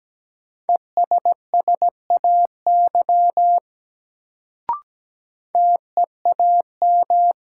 Morse code, heard as a keyed tone, about 17 wpm, 700 Hz. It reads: ESSAY TEAM
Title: Sentences from Top 1500 Words - New Word Drill Alt - 17wpm